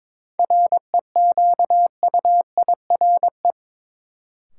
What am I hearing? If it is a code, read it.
Morse code, heard as a keyed tone, about 22 wpm, 700 Hz. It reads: REQUIRE
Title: Sentences from Top 900 Words - New Word Drill - 22wpm